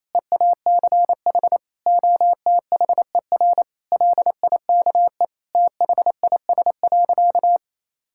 Morse code, heard as a keyed tone, about 28 wpm, 700 Hz. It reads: EACH OTHER LIKE THIS.
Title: Sentences from Top 1000 Words - Review Part 6 - 28wpm